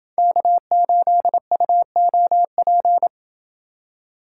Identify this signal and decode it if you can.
Morse code, heard as a keyed tone, about 26 wpm, 700 Hz. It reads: K8UOP